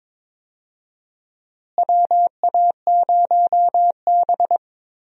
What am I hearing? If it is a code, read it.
Morse code, heard as a keyed tone, about 21 wpm, 700 Hz. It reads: WA0B
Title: USA Call Signs - 4 Characters - Repeated 3x - 22wpm